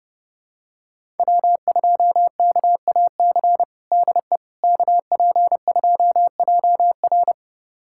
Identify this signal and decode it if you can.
Morse code, heard as a keyed tone, about 30 wpm, 700 Hz. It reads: W2KAC DE KP2JR
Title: USA Callsign DE Callsign - 5 Characters - 30wpm